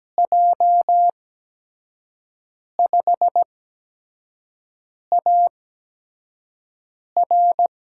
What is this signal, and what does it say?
Morse code, heard as a keyed tone, about 17 wpm, spaced as for 5 wpm, 700 Hz. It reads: J5AR
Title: USA Call Signs - 8x Character Spacing - 17wpm